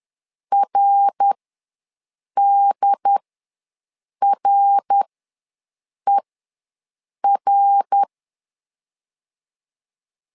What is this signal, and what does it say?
Morse code, heard as a keyed tone, about 10 wpm, spaced as for 6 wpm, 800 Hz. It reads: RDRER
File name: LEZIONI / 3 MWRO / 3 MWR ws 10wpm 2151